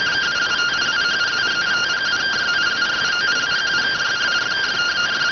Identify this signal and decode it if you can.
Two-tone teleprinter signal, about 170 Hz shift.